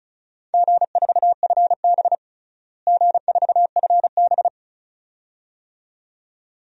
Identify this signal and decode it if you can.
Morse code, heard as a keyed tone, about 34 wpm, 700 Hz. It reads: G4FB G4FB
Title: DXCC-100 Call Signs - 4 Characters - Repeated 2x - 35wpm